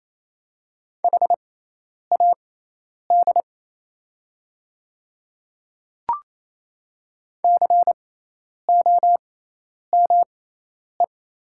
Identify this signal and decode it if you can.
Morse code, heard as a keyed tone, about 27 wpm, spaced as for 10 wpm, 700 Hz. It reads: HAD COME
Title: 6x Character Spacing - Top 100 Words - 28wpm